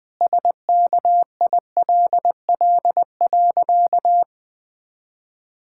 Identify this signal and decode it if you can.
Morse code, heard as a keyed tone, about 20 wpm, 700 Hz. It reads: SKILL.
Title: Pangram Challenge v2 - 20wpm